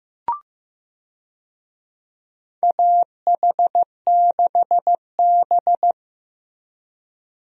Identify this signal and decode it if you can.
Morse code, heard as a keyed tone, about 15 wpm, 700 Hz. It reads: AH6B
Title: USA Call Signs - 4 Characters - Repeated 2x - 15wpm